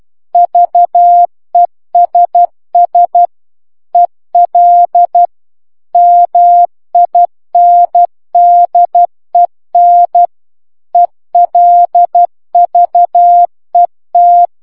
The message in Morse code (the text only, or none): VESS EL MINDEN ELVET